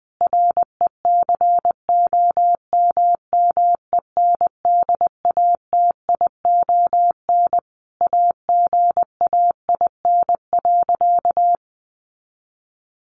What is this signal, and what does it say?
Morse code, heard as a keyed tone, about 20 wpm, 700 Hz. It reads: RECOMMENDATION AGAIN.
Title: Sentences from Top 1400 Words 20wpm